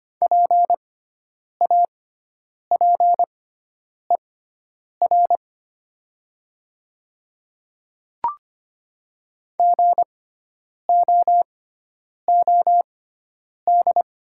Morse code, heard as a keyed tone, about 25 wpm, spaced as for 9 wpm, 690 Hz. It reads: PAPER GOOD
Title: 6x Character Spacing - Top 300 Words - 25wpm